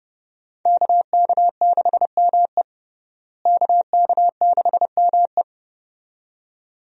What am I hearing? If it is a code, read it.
Morse code, heard as a keyed tone, about 30 wpm, 700 Hz. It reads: KK6ME KK6ME
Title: USA Call Signs - 5 Characters - Repeated 2x - 30wpm